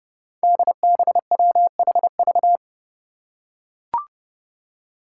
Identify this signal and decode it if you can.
Morse code, heard as a keyed tone, about 30 wpm, 700 Hz. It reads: DBWHV
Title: Five Challenging Letter-Number Combinations 30wpm